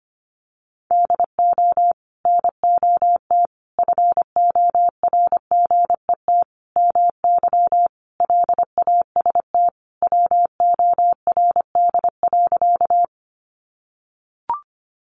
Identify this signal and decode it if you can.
Morse code, heard as a keyed tone, about 25 wpm, 700 Hz. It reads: DO NOT FORGET MY LAST WORD.
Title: Sentences from Top 1200 Words 25wpm